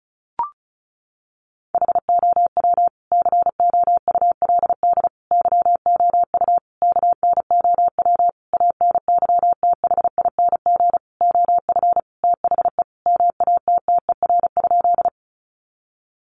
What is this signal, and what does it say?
Morse code, heard as a keyed tone, about 34 wpm, 700 Hz. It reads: HOW COULD YOU KNOW ANYTHING OF THE MATTER?
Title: Sentences from Top 1000 Words 35wpm